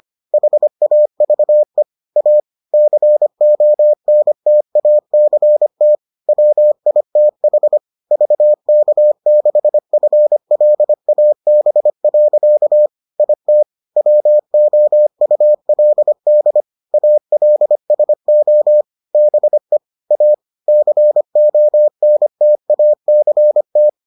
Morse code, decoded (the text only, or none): HAVE A CONTACT WITH VK6FLAB. IT WOULD ALSO BE A CONTACT